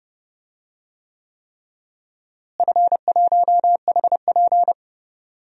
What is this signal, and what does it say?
Morse code, heard as a keyed tone, about 32 wpm, 700 Hz. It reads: F1HP
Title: DXCC-100 Call Signs - 4 Characters - Repeated 3x - 30wpm